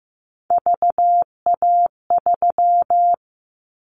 Morse code, heard as a keyed tone, about 15 wpm, 700 Hz. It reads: VA3
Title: Callsign Prefixes for Top 100 DXCC - 15wpm